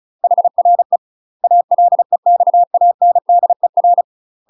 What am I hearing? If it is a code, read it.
Morse code, heard as a keyed tone, about 34 wpm, 700 Hz. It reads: HRE ALEXANDER
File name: UK MASTER QSOs TRAINING / 35WPM / UK MASTER QSOs 0000